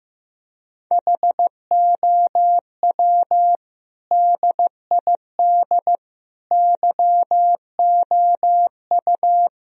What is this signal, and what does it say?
Morse code, heard as a keyed tone, about 15 wpm, 700 Hz. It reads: HOW DID YOU
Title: Sentences from Top 1000 Words - Review Part 3 - 15wpm